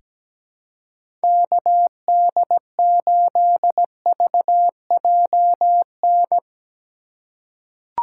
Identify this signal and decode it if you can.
Morse code, heard as a keyed tone, about 17 wpm, 700 Hz. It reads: KD8VJN
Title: USA Call Signs - 6 Characters - 3x Character Spacing - 17wpm